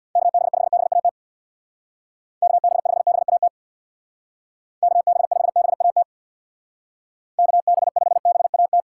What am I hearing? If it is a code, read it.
Morse code, heard as a keyed tone, about 61 wpm, 700 Hz. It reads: KB5BAT KB5BAT KB5BAT KB5BAT
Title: Speed Racing - USA Call Signs 50wpm